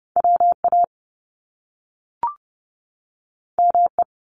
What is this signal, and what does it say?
Morse code, heard as a keyed tone, about 30 wpm, 700 Hz. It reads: WA ME
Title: Top Two Letter Combinations in Words 30wpm